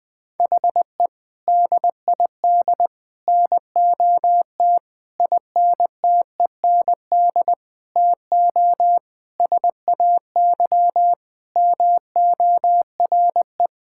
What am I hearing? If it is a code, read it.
Morse code, heard as a keyed tone, about 20 wpm, 700 Hz. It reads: HE DID NOT INTEND TO SAY MORE
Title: Sentences from Top 1600 Words 20wpm